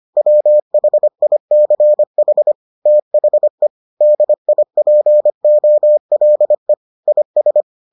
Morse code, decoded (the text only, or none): WHICH THE DIPOLE IS